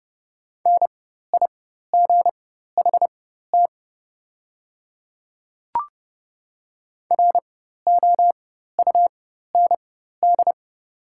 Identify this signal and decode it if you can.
Morse code, heard as a keyed tone, about 31 wpm, spaced as for 14 wpm, 700 Hz. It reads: NIGHT ROUND
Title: 4x Character Spacing - Top 300 Words - 30wpm